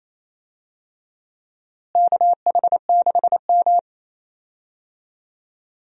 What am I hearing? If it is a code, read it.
Morse code, heard as a keyed tone, about 27 wpm, 700 Hz. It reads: KH6M
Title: USA Call Signs - 4 Characters - Repeated 2x - 28wpm